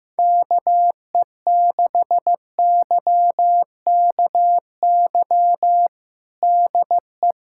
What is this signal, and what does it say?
Morse code, heard as a keyed tone, about 15 wpm, 700 Hz. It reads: KE6YKY DE
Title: USA Callsign DE Callsign - 6 Characters - 15wpm